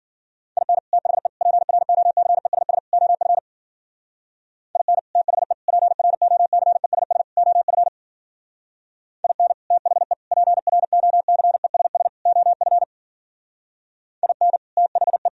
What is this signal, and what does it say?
Morse code, heard as a keyed tone, about 53 wpm, 700 Hz. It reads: IN THE PROCESS OF IN THE PROCESS OF IN THE PROCESS OF IN THE